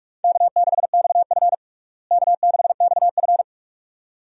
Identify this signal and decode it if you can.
Morse code, heard as a keyed tone, about 46 wpm, 700 Hz. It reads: K6XF K6XF
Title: USA Call Signs - 4 Characters - Repeated 2x - 45wpm